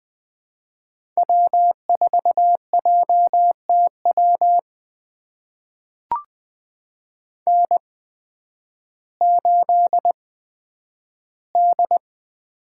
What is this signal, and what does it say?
Morse code, heard as a keyed tone, about 20 wpm, spaced as for 11 wpm, 700 Hz. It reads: W4JTW N8D
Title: USA Call Signs - 5 Characters - 8x Character Spacing - 20wpm